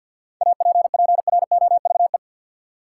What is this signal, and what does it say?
Morse code, heard as a keyed tone, about 50 wpm, 700 Hz. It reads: APPROVE
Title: Seven Letter Words - 50wpm